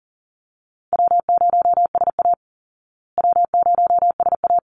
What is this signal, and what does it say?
Morse code, heard as a keyed tone, about 40 wpm, 700 Hz. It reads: W0SA W0SA
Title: Easy Call Signs Repeated 2x - 40wpm